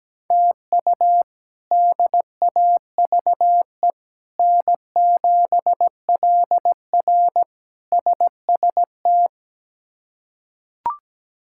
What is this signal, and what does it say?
Morse code, heard as a keyed tone, about 17 wpm, 700 Hz. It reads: TU DAVE N7LR SST
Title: K1USN SST QSO Practice - 17wpm